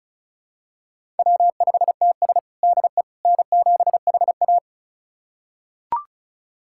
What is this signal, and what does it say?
Morse code, heard as a keyed tone, about 36 wpm, 700 Hz. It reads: W5TS DE N7HA
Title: USA Callsign DE Callsign - 4 Characters - 35wpm